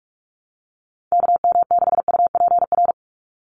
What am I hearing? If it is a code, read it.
Morse code, heard as a keyed tone, about 46 wpm, 690 Hz. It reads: KM6UPR
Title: USA Call Signs 45wpm